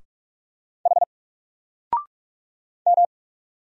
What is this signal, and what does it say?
Morse code, heard as a keyed tone, about 44 wpm, 700 Hz. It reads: H M
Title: Single Letter-Number 45wpm